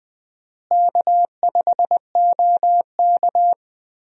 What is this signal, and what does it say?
Morse code, heard as a keyed tone, about 20 wpm, 700 Hz. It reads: K5OK
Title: USA Call Signs - 4 Characters - Repeated 2x - 20wpm